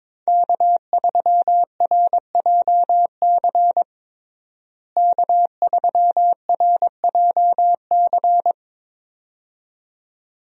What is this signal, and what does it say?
Morse code, heard as a keyed tone, about 21 wpm, 700 Hz. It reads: K3RJC K3RJC
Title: USA Call Signs - 5 Characters - Repeated 2x - 22wpm